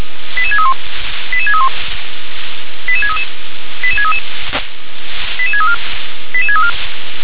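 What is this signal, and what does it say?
Five-tone selective call (ZVEI), heard as quick runs of five tones, tones of about 70 ms. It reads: 88631 88631 88633 88633 88634 88634
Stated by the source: ZVEI five-tone selective call 88730, received by SDR.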